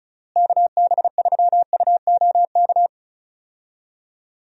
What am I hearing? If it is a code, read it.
Morse code, heard as a keyed tone, about 34 wpm, 700 Hz. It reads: KB3UOK
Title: USA Call Signs - 6 Characters - Repeated 3x - 35wpm